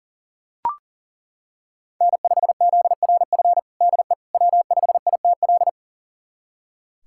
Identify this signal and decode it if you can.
Morse code, heard as a keyed tone, about 40 wpm, 700 Hz. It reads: N5ZRF DE W5ITL
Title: USA Callsign DE Callsign - 5 Characters - 40wpm